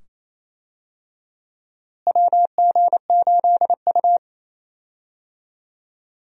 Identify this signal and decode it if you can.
Morse code, heard as a keyed tone, about 28 wpm, 700 Hz. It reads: WG8U
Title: USA Call Signs - 4 Characters - Repeated 2x - 28wpm